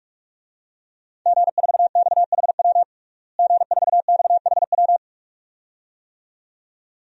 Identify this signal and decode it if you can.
Morse code, heard as a keyed tone, about 42 wpm, 700 Hz. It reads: G4XHW G4XHW
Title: DXCC-100 Call Signs - 5 Characters - Repeated 2x - 45wpm